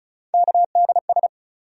Morse code, heard as a keyed tone, about 36 wpm, 700 Hz. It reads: KDS
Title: USA Call Sign Postfixes - 3 Letters 35wpm